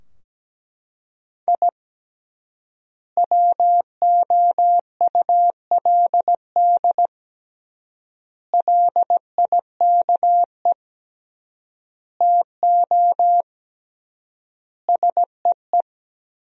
Morse code, heard as a keyed tone, about 17 wpm, 700 Hz. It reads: I WOULD LIKE TO SEE
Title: 3x Word-Spacing - Sentences from Top 200 Words - 17wpm